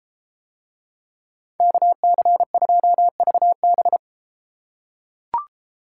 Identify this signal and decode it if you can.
Morse code, heard as a keyed tone, about 31 wpm, 700 Hz. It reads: KC2VB